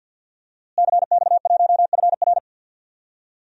Morse code, heard as a keyed tone, about 51 wpm, 700 Hz. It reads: CX1FR
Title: DXCC-100 Call Signs - 5 Characters - Repeated 2x - 50wpm